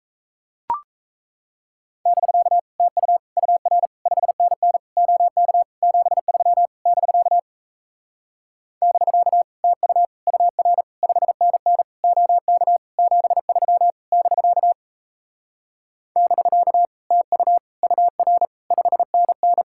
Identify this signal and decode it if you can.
Morse code, heard as a keyed tone, about 37 wpm, 700 Hz. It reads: <BK> TU UR 5NN OK 73 <BK> <BK> TU UR 5NN OK 73 <BK> <BK> TU UR 5NN